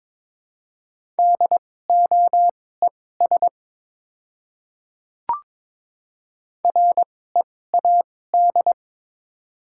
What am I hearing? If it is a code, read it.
Morse code, heard as a keyed tone, about 22 wpm, spaced as for 16 wpm, 700 Hz. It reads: DOES READ
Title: 2x Character Spacing - Top 200 Words - 22wpm